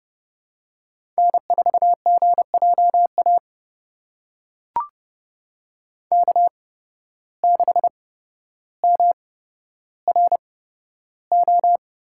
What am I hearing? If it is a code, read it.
Morse code, heard as a keyed tone, about 30 wpm, spaced as for 13 wpm, 700 Hz. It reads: N4GJA K6MRO
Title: USA Call Signs - 5 Characters - 8x Character Spacing - 30wpm